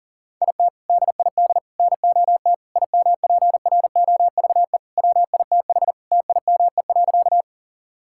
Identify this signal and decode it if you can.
Morse code, heard as a keyed tone, about 40 wpm, 690 Hz. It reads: IT DID NOT IMPROVE WITH TIME.
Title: Sentences from Top 1000 Words - Review Part 5 - 40wpm